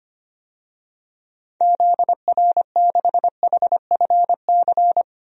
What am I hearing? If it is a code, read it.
Morse code, heard as a keyed tone, about 25 wpm, 700 Hz. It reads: ZR6HFC
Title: DXCC-100 Call Signs - 6 Characters - 25wpm